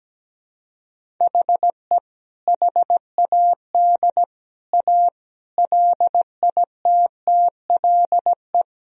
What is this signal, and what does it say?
Morse code, heard as a keyed tone, about 17 wpm, 700 Hz. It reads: HE HAD A LITTLE
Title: Sentences from Top 1000 Words - Review Part 1 - 17wpm